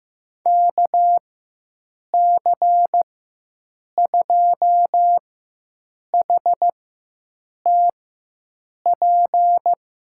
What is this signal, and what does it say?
Morse code, heard as a keyed tone, about 15 wpm, spaced as for 7 wpm, 700 Hz. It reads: KC2HTP
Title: USA Call Signs - 6 Characters - 4x Character Spacing - 15wpm